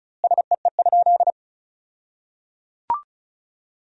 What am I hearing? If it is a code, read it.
Morse code, heard as a keyed tone, about 34 wpm, 700 Hz. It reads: SEE?
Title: 2x Word-Spacing - Sentences from Top 100 Words - 35wpm